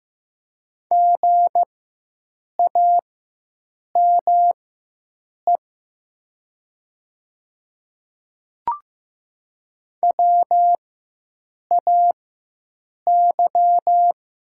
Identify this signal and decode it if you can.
Morse code, heard as a keyed tone, about 15 wpm, spaced as for 7 wpm, 690 Hz. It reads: GAME WAY
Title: 4x Character Spacing - Top 500 Words - 15wpm